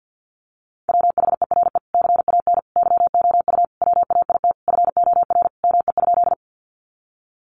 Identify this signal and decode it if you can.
Morse code, heard as a keyed tone, about 50 wpm, 700 Hz. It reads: WHERE CAN YOU WAIT FOR ME?
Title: Sentences from Top 500 Words 50wpm